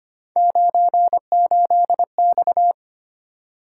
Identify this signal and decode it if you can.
Morse code, heard as a keyed tone, about 25 wpm, 700 Hz. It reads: 98X
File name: Three Letter-Number Combinations - 25wpm